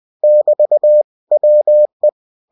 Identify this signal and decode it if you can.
Morse code, heard as a keyed tone, about 20 wpm, 600 Hz. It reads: <BT> WE
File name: cw-books / winnie-the-pooh / winnie-20wpm-page-0066